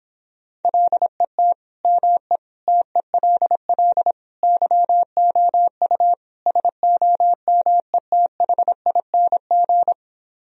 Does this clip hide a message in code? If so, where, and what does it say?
Yes: Morse code from 0.6 to 9.9 s, heard as a keyed tone, about 29 wpm, 700 Hz. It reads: LET ME TELL YOU SOMETHING